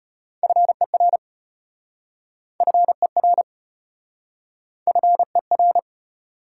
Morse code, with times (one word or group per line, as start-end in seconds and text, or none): 0.4-1.2 FER
2.6-3.4 FER
4.9-5.8 FER